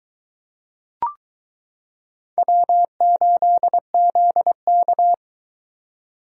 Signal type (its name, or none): Morse code